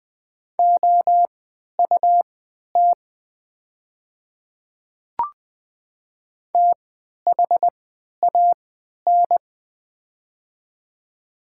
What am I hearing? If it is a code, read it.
Morse code, heard as a keyed tone, about 20 wpm, spaced as for 11 wpm, 700 Hz. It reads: OUT THAN
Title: 3x Character Spacing - Top 100 Words - 20wpm